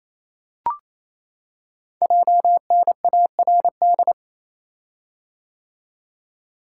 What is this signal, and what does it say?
Morse code, heard as a keyed tone, about 27 wpm, 700 Hz. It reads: JNARD
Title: Code Groups - 5 Letters - 28wpm